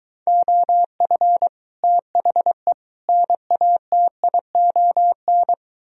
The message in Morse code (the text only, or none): OF THE NATION